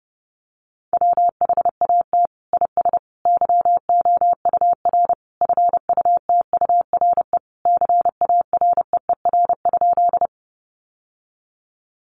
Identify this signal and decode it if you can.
Morse code, heard as a keyed tone, about 30 wpm, 700 Hz. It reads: WHAT IS YOUR FUTURE CAREER?
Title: Sentences from Top 1000 Words 30wpm